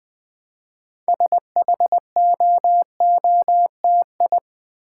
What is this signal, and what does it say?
Morse code, heard as a keyed tone, about 20 wpm, 700 Hz. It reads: SHOOTI
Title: Build A Word - 3 Letters At A Time - 8-Letter Words - 20wpm